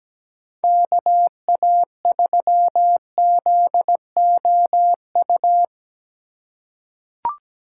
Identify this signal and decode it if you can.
Morse code, heard as a keyed tone, about 17 wpm, 700 Hz. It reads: KA3ZOU
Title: USA Call Signs - 6 Characters - 3x Character Spacing - 17wpm